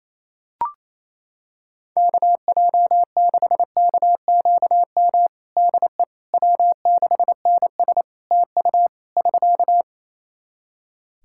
Morse code, heard as a keyed tone, about 27 wpm, 700 Hz. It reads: KJ6KQM DE W6NS TU <SK>
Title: Practice QSO - Exchange 4 - Ending - 28wpm